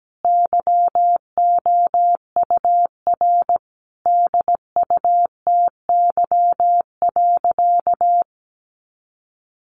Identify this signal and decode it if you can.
Morse code, heard as a keyed tone, about 17 wpm, 700 Hz. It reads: YOUR DUTY.
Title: Sentences from Top 1500 Words 17wpm